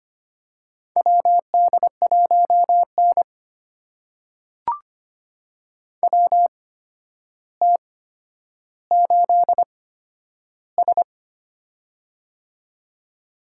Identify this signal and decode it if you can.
Morse code, heard as a keyed tone, about 25 wpm, spaced as for 11 wpm, 700 Hz. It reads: WD1N WT8S
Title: USA Call Signs - 4 Characters - 8x Character Spacing - 25wpm